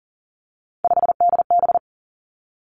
Morse code, heard as a keyed tone, about 38 wpm, 690 Hz. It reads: 5DB
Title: Three Challenging Letter-Number Combinations 40wpm